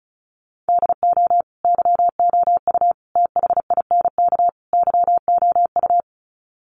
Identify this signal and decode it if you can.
Morse code, heard as a keyed tone, about 34 wpm, 700 Hz. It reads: DO YOU THINK YOU
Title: Sets of 4 Words 35wpm